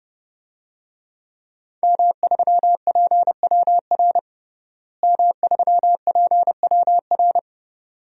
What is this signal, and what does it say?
Morse code, heard as a keyed tone, about 30 wpm, 700 Hz. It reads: M3PWR M3PWR
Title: DXCC-100 Call Signs - 5 Characters - Repeated 2x - 30wpm